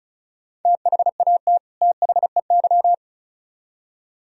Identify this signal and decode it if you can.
Morse code, heard as a keyed tone, about 33 wpm, 700 Hz. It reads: THAT THEY
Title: Speed-Racing - Sets of 2 Words 35wpm